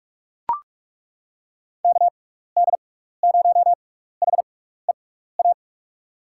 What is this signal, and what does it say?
Morse code, heard as a keyed tone, about 41 wpm, spaced as for 15 wpm, 700 Hz. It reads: KD0HEA
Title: USA Call Signs - 6 Characters - 6x Character Spacing - 45wpm